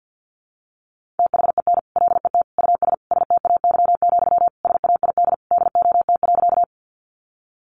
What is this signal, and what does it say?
Morse code, heard as a keyed tone, about 50 wpm, 700 Hz. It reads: THEN LET US STAY, SAID DOT.
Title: Sentences from Top 1500 Words 50wpm